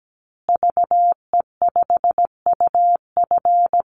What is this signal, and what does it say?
Morse code, heard as a keyed tone, about 17 wpm, 700 Hz. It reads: VE5UF
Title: Call Signs Repeated 2x - 17wpm